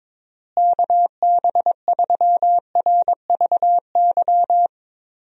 Morse code, heard as a keyed tone, about 21 wpm, 700 Hz. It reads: KB3RVY
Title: USA Call Signs - 6 Characters - Repeated 3x - 22wpm